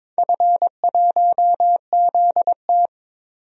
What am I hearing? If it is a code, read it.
Morse code, heard as a keyed tone, about 22 wpm, 700 Hz. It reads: F1ZT
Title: DXCC-100 Call Signs - 4 Characters - Repeated 2x - 22wpm